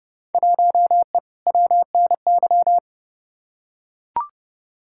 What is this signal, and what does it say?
Morse code, heard as a keyed tone, about 30 wpm, 700 Hz. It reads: 1E WNY
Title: Field Day Exchanges - 30wpm